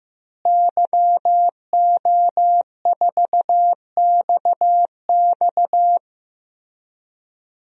Morse code, heard as a keyed tone, about 15 wpm, 700 Hz. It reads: YO4XX
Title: DXCC-100 Call Signs - 5 Characters - Repeated 3x - 15wpm